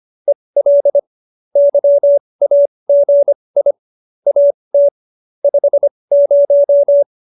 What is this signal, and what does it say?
Morse code, heard as a keyed tone, about 25 wpm, spaced as for 20 wpm, 570 Hz. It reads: EL YAGI AT 50